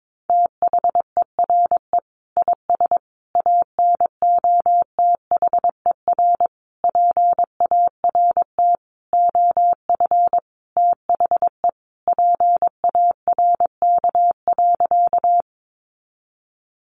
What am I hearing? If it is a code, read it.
Morse code, heard as a keyed tone, about 22 wpm, 700 Hz. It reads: THERE IS ANOTHER PART OF THE PARK.
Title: Sentences from Top 1300 Words 22wpm